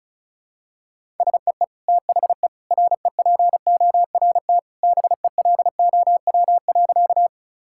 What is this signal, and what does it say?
Morse code, heard as a keyed tone, about 36 wpm, 700 Hz. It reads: SEE THE REPORT BELOW.
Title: Sentences from Top 1000 Words - Review Part 6 - 35wpm